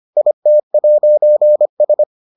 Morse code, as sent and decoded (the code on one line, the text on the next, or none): .. - .----. ...
IT'S